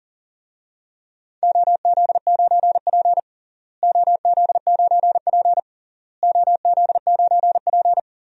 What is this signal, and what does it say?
Morse code, heard as a keyed tone, about 37 wpm, 700 Hz. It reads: OZ9P OZ9P OZ9P
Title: DXCC-100 Call Signs - 4 Characters - Repeated 3x - 40wpm